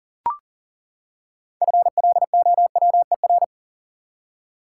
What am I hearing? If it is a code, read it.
Morse code, heard as a keyed tone, about 40 wpm, 700 Hz. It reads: FLOWER
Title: Six Letter Words - 40wpm